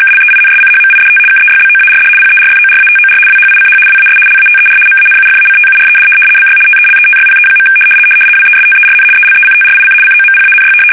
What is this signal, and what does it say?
RTTY, about 75 baud. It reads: KLEINSCHMIDT KEYBOARD PERFORATOR WHICH WAS LATER MANUFACTURED BY TELETYPE. WESTERN ELECTRIC COMPANY
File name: fsk850-75B / teletype-corp